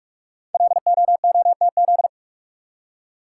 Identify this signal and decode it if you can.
Morse code, heard as a keyed tone, about 44 wpm, 700 Hz. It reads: LOOTZ
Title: Code Groups - 5 Letters - 45wpm